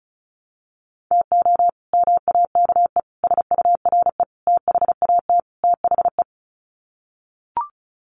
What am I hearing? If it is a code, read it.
Morse code, heard as a keyed tone, about 34 wpm, 700 Hz. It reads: TO MAKE SURE THAT THE
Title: Sets of 5 Words 35wpm